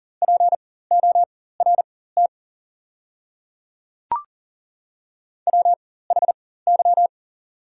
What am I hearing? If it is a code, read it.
Morse code, heard as a keyed tone, about 40 wpm, spaced as for 19 wpm, 700 Hz. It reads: PORT WHY
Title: 4x Character Spacing - Top 200 Words - 40wpm